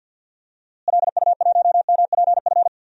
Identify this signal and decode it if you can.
Morse code, heard as a keyed tone, about 48 wpm, 700 Hz. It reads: LU1MPF
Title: DXCC-100 Call Signs - 6 Characters - Repeated 2x - 50wpm